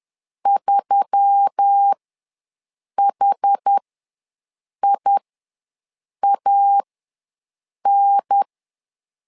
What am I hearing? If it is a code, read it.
Morse code, heard as a keyed tone, about 10 wpm, spaced as for 6 wpm, 800 Hz. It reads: 3HIAN
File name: LEZIONI / 4 Q1S3 ws 10wpm 2437